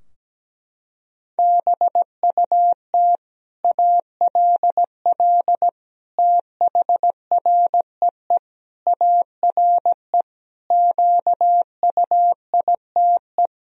Morse code, heard as a keyed tone, about 17 wpm, 700 Hz. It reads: BUT ALL THREE ARE QUITE
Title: Sentences from Top 1000 Words - Review Part 3 - 17wpm